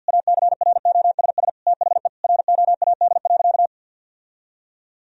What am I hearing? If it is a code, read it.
Morse code, heard as a keyed tone, about 52 wpm, 700 Hz. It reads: ACROSS THE ROAD.
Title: Sentences from Top 1000 Words - Review Part 1 - 50wpm